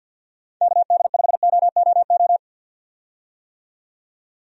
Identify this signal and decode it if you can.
Morse code, heard as a keyed tone, about 46 wpm, 700 Hz. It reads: KD5OOO
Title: USA Call Signs - 6 Characters - 50wpm